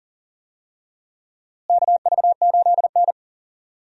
Morse code, heard as a keyed tone, about 40 wpm, 700 Hz. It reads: KV8N
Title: USA Call Signs - 4 Characters - Repeated 3x - 40wpm